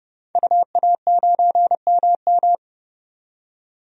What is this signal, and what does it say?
Morse code, heard as a keyed tone, about 30 wpm, 700 Hz. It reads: UA9MM
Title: DXCC-100 Call Signs - 5 Characters - 30wpm